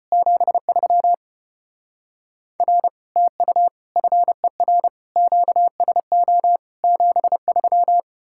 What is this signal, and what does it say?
Morse code, heard as a keyed tone, about 33 wpm, 700 Hz. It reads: 73 R TU FER QSO 73